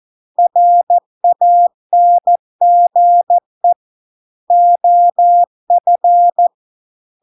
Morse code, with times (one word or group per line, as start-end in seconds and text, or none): 0.4-3.7 RANGE
4.5-6.5 OF